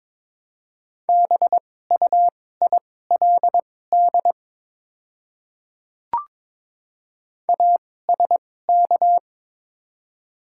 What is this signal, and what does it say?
Morse code, heard as a keyed tone, about 23 wpm, spaced as for 16 wpm, 690 Hz. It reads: BUILD ASK
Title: 2x Character Spacing - Top 200 Words - 22wpm